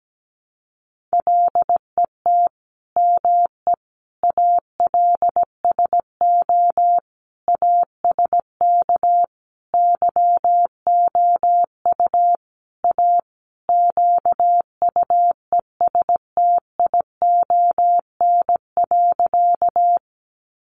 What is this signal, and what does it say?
Morse code, heard as a keyed tone, about 17 wpm, 700 Hz. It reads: LET ME ALSO ASK YOU A QUESTION.
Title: Sentences from Top 400 Words 17wpm